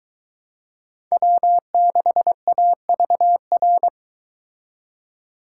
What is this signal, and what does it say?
Morse code, heard as a keyed tone, about 23 wpm, 700 Hz. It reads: W6AVR